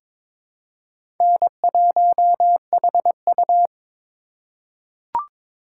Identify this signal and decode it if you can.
Morse code, heard as a keyed tone, about 21 wpm, 700 Hz. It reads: N1HU
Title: USA Call Signs - 4x Character Spacing - 22wpm